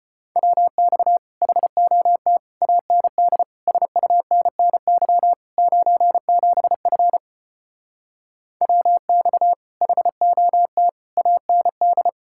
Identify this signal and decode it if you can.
Morse code, heard as a keyed tone, about 33 wpm, 700 Hz. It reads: WX HOT AND SUNNY 97F WX HOT AND